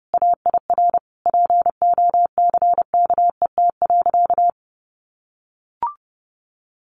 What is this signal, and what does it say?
Morse code, heard as a keyed tone, about 30 wpm, 700 Hz. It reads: AIR POCKET.
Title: A Sound Of Thunder 30wpm